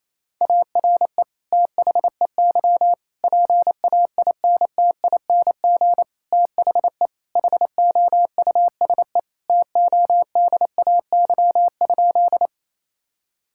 Morse code, heard as a keyed tone, about 28 wpm, 700 Hz. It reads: ARE THEY PAINTING THE HOUSE TODAY?
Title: Sentences from Top 1000 Words - Review Part 6 - 28wpm